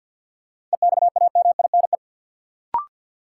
Morse code, heard as a keyed tone, about 53 wpm, 700 Hz. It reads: EXAMINE
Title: Sentences from Top 1600 Words - New Word Drill Alt - 50wpm